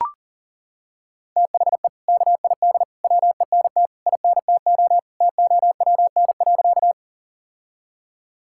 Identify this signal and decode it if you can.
Morse code, heard as a keyed tone, about 40 wpm, 700 Hz. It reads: THE KID WENT INTO TOWN.
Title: Sentences from Top 1600 Words 40wpm